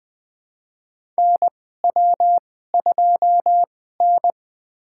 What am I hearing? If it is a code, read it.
Morse code, heard as a keyed tone, about 20 wpm, spaced as for 14 wpm, 700 Hz. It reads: NW2N
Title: USA Call Signs - 4 Characters - 2x Character Spacing - 20wpm